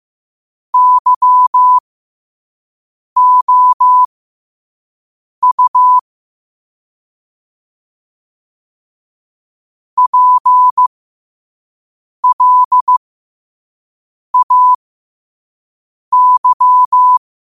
Morse code, heard as a keyed tone, about 15 wpm, spaced as for 5 wpm, 1000 Hz. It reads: YOU PLAY
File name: qotdfiles / 5wpm